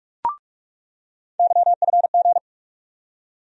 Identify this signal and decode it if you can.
Morse code, heard as a keyed tone, about 48 wpm, 700 Hz. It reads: YFG
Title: USA Call Sign Postfixes - 3 Letters 45wpm